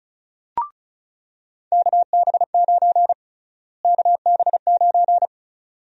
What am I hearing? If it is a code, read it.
Morse code, heard as a keyed tone, about 34 wpm, 700 Hz. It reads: KB9 KB9
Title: USA Call Sign Prefixes - Repeated 2x - 35wpm